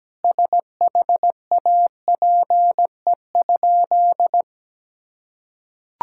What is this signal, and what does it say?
Morse code, heard as a keyed tone, about 17 wpm, 700 Hz. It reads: SHAPE?
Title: Sentences from Top 1000 Words - Review Part 6 - 17wpm